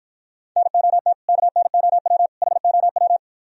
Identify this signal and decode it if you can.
Morse code, heard as a keyed tone, about 52 wpm, 700 Hz. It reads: NOT KNOW HOW